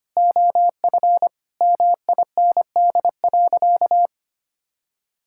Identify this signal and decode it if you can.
Morse code, heard as a keyed tone, about 25 wpm, 700 Hz. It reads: OF MIND.
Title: Sentences from Top 1000 Words - Review Part 6 - 25wpm